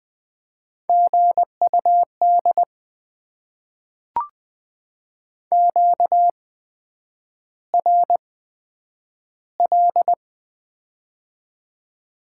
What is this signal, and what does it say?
Morse code, heard as a keyed tone, about 20 wpm, spaced as for 9 wpm, 700 Hz. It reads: GUD QRL
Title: Top 20 QSO Elements - 8x Character Spacing - 20wpm